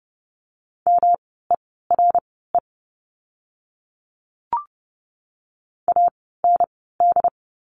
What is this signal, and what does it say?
Morse code, heard as a keyed tone, about 30 wpm, spaced as for 17 wpm, 700 Hz. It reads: MERE AND
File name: course / 30:10wpm / 108 Lesson 18f - Letter m - Words 30wpm